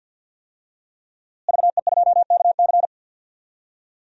Morse code, heard as a keyed tone, about 53 wpm, 700 Hz. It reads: VE2KC
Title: DXCC-100 Call Signs - 5 Characters - Repeated 3x - 50wpm